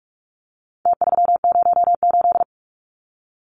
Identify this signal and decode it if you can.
Morse code, heard as a keyed tone, about 44 wpm, 700 Hz. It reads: T308
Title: Numbers - Contest Style - 4-Digit 45wpm